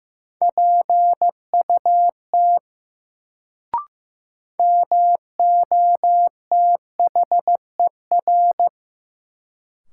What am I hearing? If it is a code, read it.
Morse code, heard as a keyed tone, about 15 wpm, 700 Hz. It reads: PUT MOTHER
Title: Sentences from Top 200 Words - New Word Drill - 15wpm